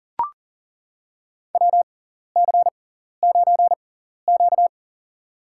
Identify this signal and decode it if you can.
Morse code, heard as a keyed tone, about 38 wpm, spaced as for 14 wpm, 700 Hz. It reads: WC9Q
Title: USA Call Signs - 4 Characters - 6x Character Spacing - 40wpm